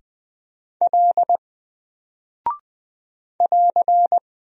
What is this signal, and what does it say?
Morse code, heard as a keyed tone, about 20 wpm, 700 Hz. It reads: L <AR>
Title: Single Character (Letter:Number:Punctuation:Prosign) 20wpm